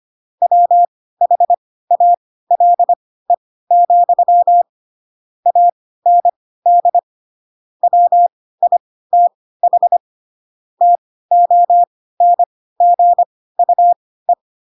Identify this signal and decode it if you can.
Morse code, heard as a keyed tone, about 25 wpm, spaced as for 16 wpm, 700 Hz. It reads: WHALE, AND WITH TONGUE